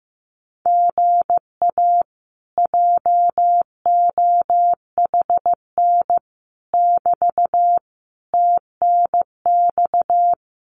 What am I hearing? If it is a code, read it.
Morse code, heard as a keyed tone, about 15 wpm, 700 Hz. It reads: GA JOHN <BT> TNX
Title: Common QSO Phrases Encore 15wpm